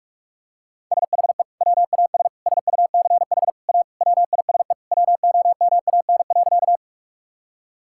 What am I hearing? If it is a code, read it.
Morse code, heard as a keyed tone, about 42 wpm, 700 Hz. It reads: SHE WAS SUCH A WISE WOMAN.
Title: Sentences from Top 1600 Words 45wpm